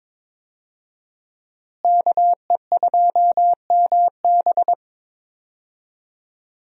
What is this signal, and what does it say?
Morse code, heard as a keyed tone, about 22 wpm, 700 Hz. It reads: KE2MB
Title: USA Call Signs - 5 Characters - Repeated 2x - 22wpm